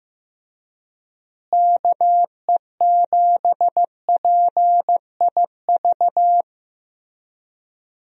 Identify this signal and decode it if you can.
Morse code, heard as a keyed tone, about 15 wpm, 700 Hz. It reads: KE7PIV
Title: USA Call Signs - 6 Characters - Repeated 3x - 15wpm